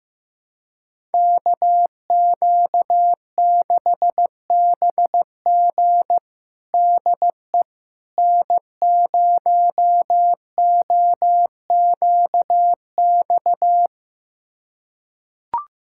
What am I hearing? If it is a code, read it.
Morse code, heard as a keyed tone, about 15 wpm, 700 Hz. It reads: KQ6BG DE N0OQX
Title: USA Callsign DE Callsign - 5 Characters - 15wpm